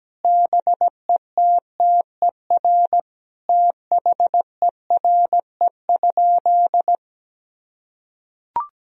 Morse code, heard as a keyed tone, about 17 wpm, 700 Hz. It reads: BETTER THERE?
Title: Sentences from Top 1000 Words - Review Part 4 - 17wpm